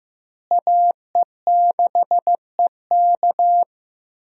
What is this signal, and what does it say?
Morse code, heard as a keyed tone, about 15 wpm, 700 Hz. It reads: AE6EK
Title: USA Call Signs - 5 Characters - Repeated 3x - 15wpm